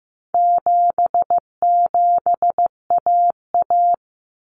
Morse code, heard as a keyed tone, about 15 wpm, 700 Hz. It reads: 77AA
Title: Common Cut Numbers - 4-Digit 15wpm